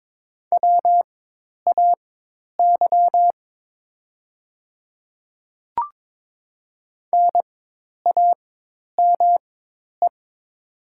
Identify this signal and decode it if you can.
Morse code, heard as a keyed tone, about 22 wpm, spaced as for 10 wpm, 700 Hz. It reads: WAY NAME
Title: 4x Character Spacing - Top 200 Words - 22wpm